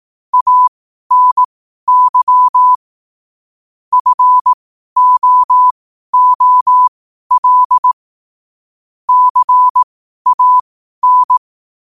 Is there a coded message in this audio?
Yes: Morse code from 0.3 to 11.4 s, heard as a keyed tone, about 18 wpm, spaced as for 12 wpm, 1000 Hz. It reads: ANY FOOL CAN